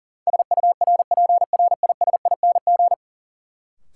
Morse code, heard as a keyed tone, about 40 wpm, 700 Hz. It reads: SURPRISING
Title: Sets of 5 Words - New Word Drill - 40wpm